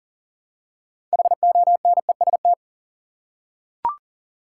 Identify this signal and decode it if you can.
Morse code, heard as a keyed tone, about 40 wpm, 700 Hz. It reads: HONEST
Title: Six Letter Words - 40wpm